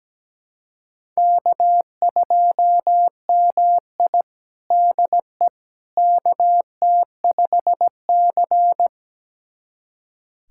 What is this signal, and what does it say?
Morse code, heard as a keyed tone, about 17 wpm, 700 Hz. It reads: K2MI DE KT5C
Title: USA Callsign DE Callsign - 4 Characters - 17wpm